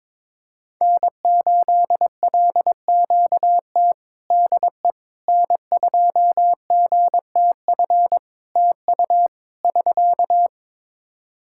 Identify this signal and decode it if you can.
Morse code, heard as a keyed tone, about 22 wpm, 700 Hz. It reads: N8LQT DE N2GTF TU <SK>
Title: K1USN SST QSO Practice 22wpm